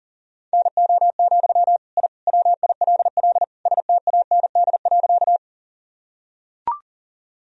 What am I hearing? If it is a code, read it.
Morse code, heard as a keyed tone, about 40 wpm, 700 Hz. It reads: NO, I WILL STAND.
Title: Sentences from Top 1000 Words - Review Part 1 - 40wpm